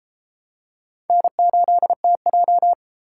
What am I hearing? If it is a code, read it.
Morse code, heard as a keyed tone, about 29 wpm, 700 Hz. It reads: N8TJ